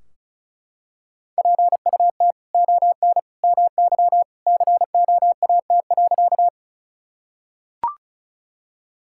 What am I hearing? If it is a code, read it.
Morse code, heard as a keyed tone, about 34 wpm, 700 Hz. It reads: PUT ON MY COAT.
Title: Sentences from Top 1600 Words 35wpm